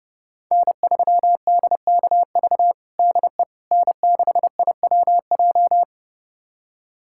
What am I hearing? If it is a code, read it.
Morse code, heard as a keyed tone, about 30 wpm, 700 Hz. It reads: N3DKV DE N6IWJ
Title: USA Callsign DE Callsign - 5 Characters - 30wpm